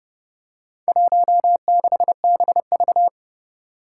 Morse code, heard as a keyed tone, about 30 wpm, 700 Hz. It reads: 16BV
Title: Four Challenging Letter-Number Combinations 30wpm